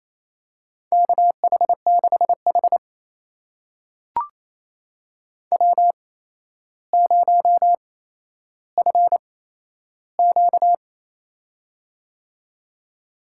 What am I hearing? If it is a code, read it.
Morse code, heard as a keyed tone, about 28 wpm, spaced as for 12 wpm, 700 Hz. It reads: KH6H W0FQ
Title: USA Call Signs - 4 Characters - 8x Character Spacing - 28wpm